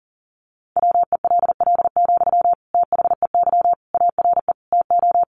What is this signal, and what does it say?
Morse code, heard as a keyed tone, about 40 wpm, 700 Hz. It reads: WELL, THEY ARE TO
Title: Sentences from Top 1500 Words 40wpm